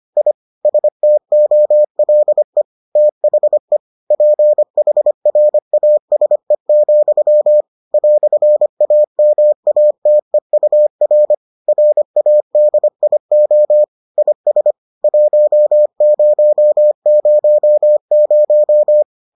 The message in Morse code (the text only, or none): I STOLE THE PHRASE, "AMATEUR RADIO IS 1000